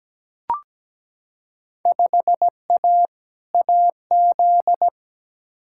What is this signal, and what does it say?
Morse code, heard as a keyed tone, about 17 wpm, 700 Hz. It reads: 5A AZ
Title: Field Day Exchanges - 17wpm